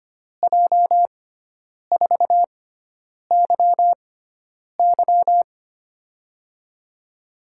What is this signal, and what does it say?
Morse code, heard as a keyed tone, about 25 wpm, spaced as for 9 wpm, 700 Hz. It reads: J4YY
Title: USA Call Signs - 6x Character Spacing - 25wpm